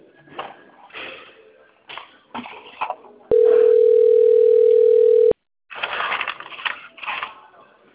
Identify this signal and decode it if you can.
Ringback tone (North America), 440 + 480 Hz.
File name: out- - - - - .9471